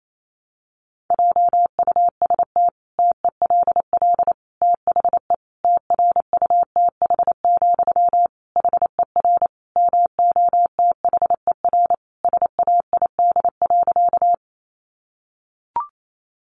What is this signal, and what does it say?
Morse code, heard as a keyed tone, about 28 wpm, 700 Hz. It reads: JUST TELL THE TRUTH, HER MOTHER SAID.
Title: Sentences from Top 800 Words 28wpm